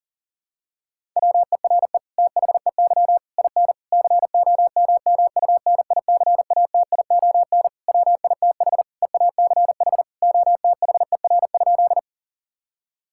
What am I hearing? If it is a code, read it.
Morse code, heard as a keyed tone, about 37 wpm, 700 Hz. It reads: WERE THEY IN COMMUNICATION WITH EACH OTHER?
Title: Sentences from Top 1000 Words - Review Part 6 - 40wpm